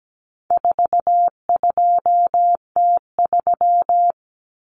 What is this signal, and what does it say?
Morse code, heard as a keyed tone, about 17 wpm, 700 Hz. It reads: 42T3
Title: Common Cut Numbers - 4-Digit 17wpm